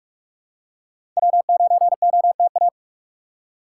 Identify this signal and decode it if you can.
Morse code, heard as a keyed tone, about 46 wpm, 700 Hz. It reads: W9OTA
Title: USA Call Signs - 2x Character Spacing - 45wpm